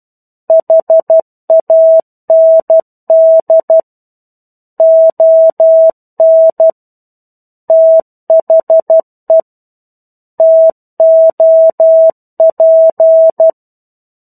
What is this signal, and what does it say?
Morse code, heard as a keyed tone, about 12 wpm, 650 Hz. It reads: HAND ON THE TOP
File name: 01-Chapter-01 12 WPM